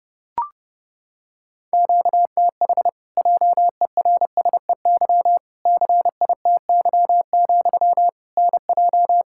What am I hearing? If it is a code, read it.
Morse code, heard as a keyed tone, about 32 wpm, 700 Hz. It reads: QTH JERSEY CITY, NJ
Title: Practice QSO - Exchange 1 - Beginning - 30wpm